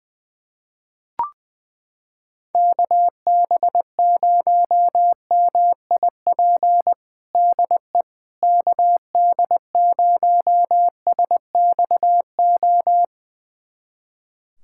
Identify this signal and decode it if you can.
Morse code, heard as a keyed tone, about 21 wpm, 700 Hz. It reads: KB0MIP DE KD0SXO
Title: USA Callsign DE Callsign 20wpm